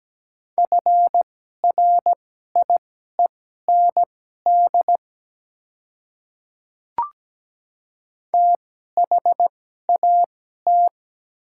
Morse code, heard as a keyed tone, about 17 wpm, spaced as for 12 wpm, 700 Hz. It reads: FRIEND THAT
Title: 2x Character Spacing - Top 300 Words - 17wpm